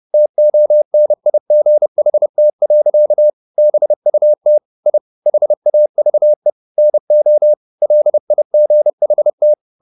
Morse code, heard as a keyed tone, about 30 wpm, 600 Hz. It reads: TONIGHT. BUT I HAVE NO LIGHT
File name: the old man and the sea / the old man and the sea 1 30wpm